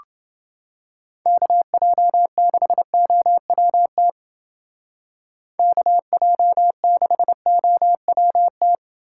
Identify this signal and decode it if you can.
Morse code, heard as a keyed tone, about 28 wpm, 700 Hz. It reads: KJ6OWT KJ6OWT